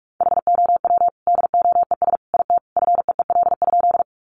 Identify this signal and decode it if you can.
Morse code, heard as a keyed tone, about 44 wpm, 700 Hz. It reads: HOW DOES IT FEEL?
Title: Sentences from Top 400 Words 45wpm